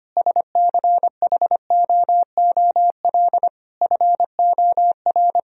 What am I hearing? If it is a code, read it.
Morse code, heard as a keyed tone, about 25 wpm, 700 Hz. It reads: SCHOOL FOR
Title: Sentences from Top 1000 Words - Review Part 4 - 25wpm